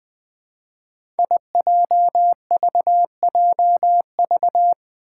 Morse code, heard as a keyed tone, about 20 wpm, 700 Hz. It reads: IJVJV
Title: Five Challenging Letter-Number Combinations 20wpm